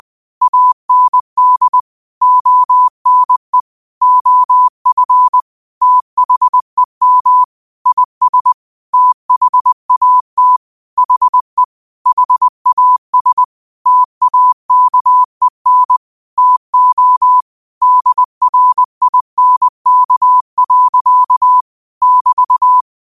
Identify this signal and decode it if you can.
Morse code, heard as a keyed tone, about 20 wpm, 1000 Hz. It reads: AND ONE OF THEM IS THAT HE HAS TAKEN TO DRINK. <BT>
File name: qotdfiles / 20wpm